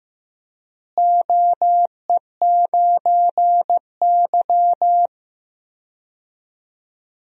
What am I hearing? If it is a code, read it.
Morse code, heard as a keyed tone, about 16 wpm, 700 Hz. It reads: OE9Y
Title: DXCC-100 Call Signs - 4 Characters - Repeated 2x - 15wpm